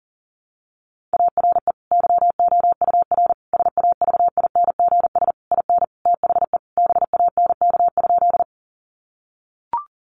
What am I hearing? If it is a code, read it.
Morse code, heard as a keyed tone, about 37 wpm, 700 Hz. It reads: ARE YOUR SAVINGS IN THE BANK?
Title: Sentences from Top 1200 Words 40wpm